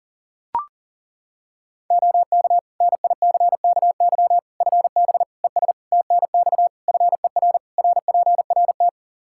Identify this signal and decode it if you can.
Morse code, heard as a keyed tone, about 40 wpm, 700 Hz. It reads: OK NICKY FB ES TNX FER RPRT
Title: Practice QSO - Exchange 2 - Info - 40wpm